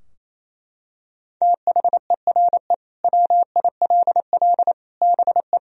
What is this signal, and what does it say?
Morse code, heard as a keyed tone, about 28 wpm, 700 Hz. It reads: THERE WILL BE
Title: Sentences from Top 1000 Words - Review Part 3 - 28wpm